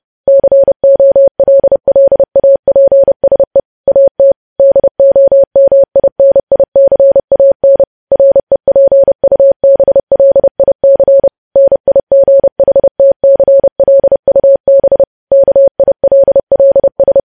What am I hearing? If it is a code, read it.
Morse code, heard as a keyed tone, about 30 wpm, 550 Hz. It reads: COLLAPSE AT DOMINICAN REPUBLIC NIGHTCLUB KILLS
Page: news headlines in morse code at 30 wpm